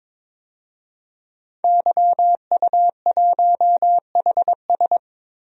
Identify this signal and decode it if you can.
Morse code, heard as a keyed tone, about 22 wpm, 700 Hz. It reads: YU1HS
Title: DXCC-100 Call Signs - 5 Characters - Repeated 2x - 22wpm